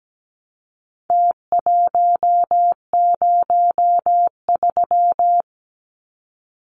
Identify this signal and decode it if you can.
Morse code, heard as a keyed tone, about 17 wpm, 700 Hz. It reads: T103
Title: Numbers - Contest Style - 4-Digit 17wpm